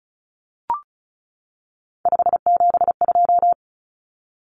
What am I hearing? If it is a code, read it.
Morse code, heard as a keyed tone, about 34 wpm, 700 Hz. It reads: 572
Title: Common Cut Numbers - 3-Digit 35wpm